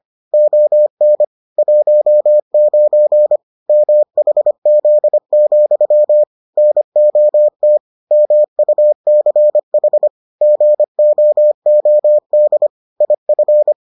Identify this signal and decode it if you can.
Morse code, heard as a keyed tone, about 25 wpm, 600 Hz. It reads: ON 19 MHZ, NOT MUCH GOOD IF